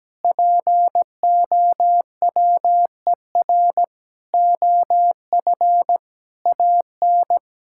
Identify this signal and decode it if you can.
Morse code, heard as a keyed tone, about 17 wpm, 700 Hz. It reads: POWER OF AN
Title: Sentences from Top 1000 Words - Review Part 5 - 17wpm